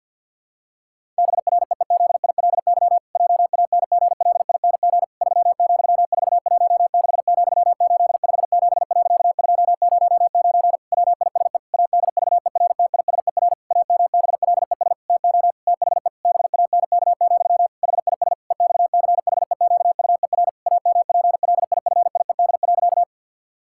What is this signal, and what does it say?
Morse code, heard as a keyed tone, about 50 wpm, 700 Hz. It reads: BREEZILY JANGLING 3,416,857.209 WISE ADVERTISER AMBLES TO THE BANK, HIS EXCHEQUER AMPLIFIED.
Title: Pangram Challenge v2 - 50wpm